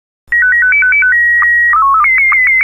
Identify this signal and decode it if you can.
SSTV (slow-scan television), Scottie 1.